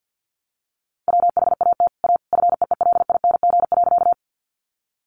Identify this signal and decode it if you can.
Morse code, heard as a keyed tone, about 48 wpm, 700 Hz. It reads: WHAT A FEELING.
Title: Sentences from Top 1200 Words 50wpm